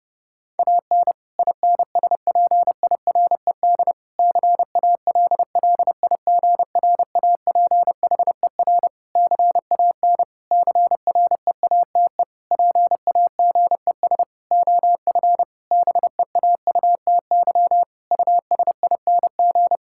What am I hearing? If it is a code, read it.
Morse code, heard as a keyed tone, about 30 wpm, 700 Hz. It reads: AN INSPIRED CALLIGRAPHER CAN CREATE PAGES OF BEAUTY USING
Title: Pangram Challenge v2 - 30wpm